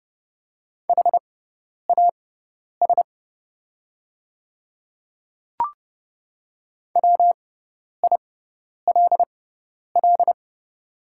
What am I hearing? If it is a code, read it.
Morse code, heard as a keyed tone, about 30 wpm, spaced as for 10 wpm, 700 Hz. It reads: HAS WILL
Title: 6x Character Spacing - Top 100 Words - 30wpm